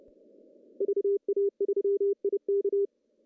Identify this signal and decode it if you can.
Morse code, heard as a keyed tone, about 29 wpm, 400 Hz. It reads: VA3IK
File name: CW / cqww-cw-k5tr